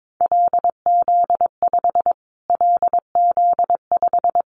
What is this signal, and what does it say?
Morse code, heard as a keyed tone, about 22 wpm, 700 Hz. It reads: LZ5 LZ5
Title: Callsign Prefixes for Top 25 DXCC - Repeated 3x - 22wpm